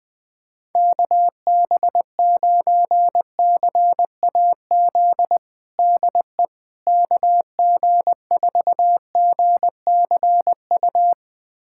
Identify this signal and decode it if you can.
Morse code, heard as a keyed tone, about 21 wpm, 700 Hz. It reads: KB9CAZ DE KG4GCU
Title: USA Callsign DE Callsign 20wpm